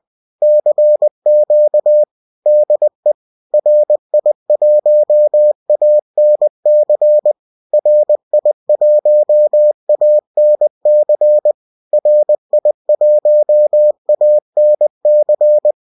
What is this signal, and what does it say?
Morse code, heard as a keyed tone, about 20 wpm, 600 Hz. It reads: CQ DE RI1ANC RI1ANC RI1ANC